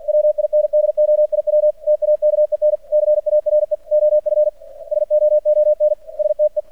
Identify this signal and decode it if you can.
Morse code, heard as a keyed tone, about 46 wpm, 600 Hz. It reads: WINDOIW TAKEN CARE OF SOON STE
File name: archived / CRSnet